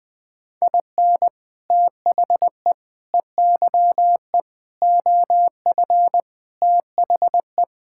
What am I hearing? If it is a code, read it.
Morse code, heard as a keyed tone, about 21 wpm, 700 Hz. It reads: IN THE EYE OF THE
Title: Sets of 6 Words - 20wpm